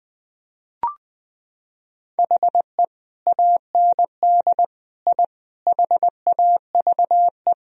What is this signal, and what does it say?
Morse code, heard as a keyed tone, about 20 wpm, 700 Hz. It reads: HE AND I HAVE
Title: Sentences from Top 1000 Words - Review Part 3 - 20wpm